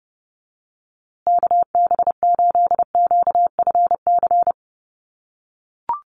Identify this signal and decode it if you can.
Morse code, heard as a keyed tone, about 30 wpm, 700 Hz. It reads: KB8QFC
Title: USA Call Signs Repeated 2x 30wpm